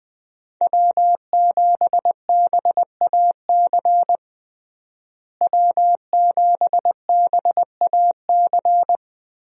Morse code, heard as a keyed tone, about 21 wpm, 700 Hz. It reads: W7BAC W7BAC
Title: USA Call Signs - 5 Characters - Repeated 3x - 20wpm